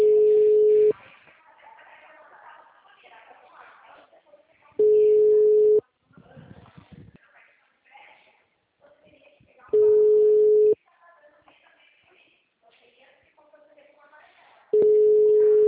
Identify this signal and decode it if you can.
Ringback tone (Europe, 425 Hz), with speech.
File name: out- -2011- - - .75203